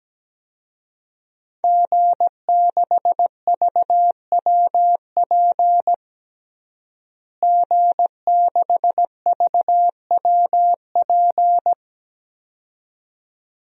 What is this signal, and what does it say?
Morse code, heard as a keyed tone, about 17 wpm, 700 Hz. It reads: G6VWP G6VWP
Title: DXCC-100 Call Signs - 5 Characters - Repeated 2x - 17wpm